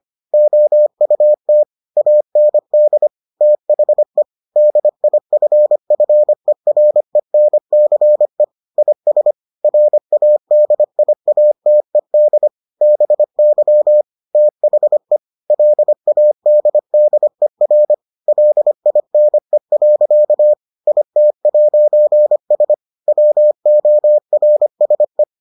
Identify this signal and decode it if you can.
Morse code, heard as a keyed tone, about 25 wpm, 600 Hz. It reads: OUT AND THE DIFFERENCE IS RADIATED BY THE LADDER LINE. IT'S WORSE